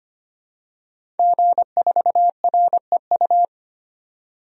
Morse code, heard as a keyed tone, about 25 wpm, 700 Hz. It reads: G4REU
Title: DXCC-100 Call Signs - 5 Characters - 25wpm